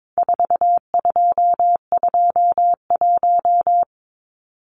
Morse code, heard as a keyed tone, about 22 wpm, 700 Hz. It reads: 4221
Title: Numbers - 4-Digit 22wpm